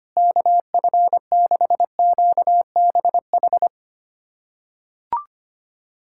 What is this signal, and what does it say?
Morse code, heard as a keyed tone, about 25 wpm, 700 Hz. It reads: KF6QBH
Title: USA Callsign DE Callsign 25wpm